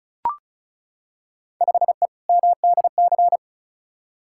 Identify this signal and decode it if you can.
Morse code, heard as a keyed tone, about 36 wpm, 700 Hz. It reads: 5E MDC
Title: Field Day Exchanges - 35wpm